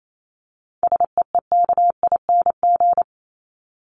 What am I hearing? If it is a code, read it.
Morse code, heard as a keyed tone, about 28 wpm, 700 Hz. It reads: SEEKING
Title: ING Suffix - Top 100 Words with Base Words 28wpm